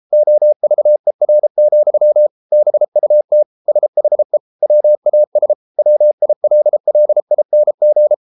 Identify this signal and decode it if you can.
Morse code, heard as a keyed tone, about 33 wpm, 600 Hz. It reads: OVER, BUT SHE WAS WILLING